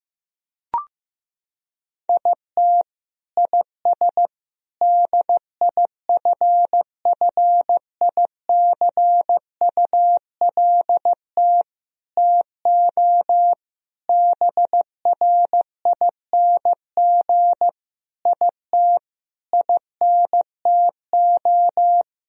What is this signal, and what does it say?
Morse code, heard as a keyed tone, about 15 wpm, 700 Hz. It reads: IT IS DIFFICULT TO BRING IT INTO
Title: Sentences from Top 1000 Words - Review Part 4 - 15wpm